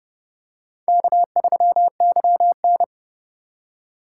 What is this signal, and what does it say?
Morse code, heard as a keyed tone, about 30 wpm, 700 Hz. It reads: K3YN
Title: USA Call Signs - 4x Character Spacing - 30wpm